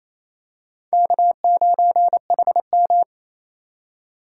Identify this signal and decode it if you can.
Morse code, heard as a keyed tone, about 28 wpm, 700 Hz. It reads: K9HM
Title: USA Call Signs - 4 Characters - 4x Character Spacing - 28wpm